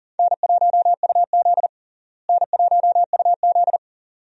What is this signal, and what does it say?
Morse code, heard as a keyed tone, about 40 wpm, 700 Hz. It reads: N1UZ N1UZ
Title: USA Call Signs - 4 Characters - Repeated 2x - 40wpm